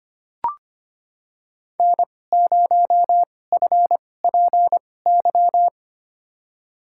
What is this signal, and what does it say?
Morse code, heard as a keyed tone, about 25 wpm, spaced as for 18 wpm, 700 Hz. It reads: N0FPY
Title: USA Call Signs - 5 Characters - 2x Character Spacing - 25wpm